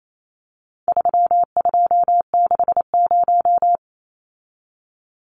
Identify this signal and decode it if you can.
Morse code, heard as a keyed tone, about 28 wpm, 700 Hz. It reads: 3260
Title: Numbers - 4-Digit 28wpm